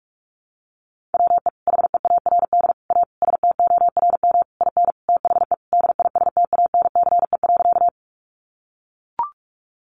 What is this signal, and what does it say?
Morse code, heard as a keyed tone, about 44 wpm, 700 Hz. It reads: WE HEARD A STORM IN THE DISTANCE.
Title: Sentences from Top 1500 Words 45wpm